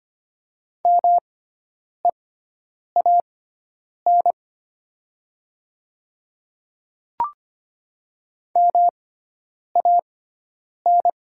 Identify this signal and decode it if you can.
Morse code, heard as a keyed tone, about 25 wpm, spaced as for 9 wpm, 700 Hz. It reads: MEAN MAN
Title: 6x Character Spacing - Top 200 Words - 25wpm